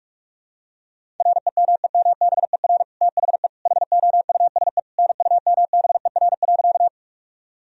Morse code, heard as a keyed tone, about 44 wpm, 700 Hz. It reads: REMEMBER THE HOUSE NUMBER.
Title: Sentences from Top 1000 Words - Review Part 1 - 45wpm